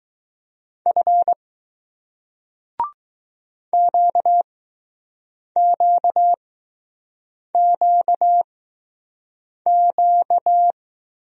Morse code, heard as a keyed tone, about 20 wpm, spaced as for 6 wpm, 700 Hz. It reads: F QQQQ